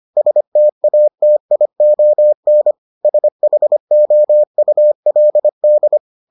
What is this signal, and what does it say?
Morse code, heard as a keyed tone, about 25 wpm, 600 Hz. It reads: STATION SHOULD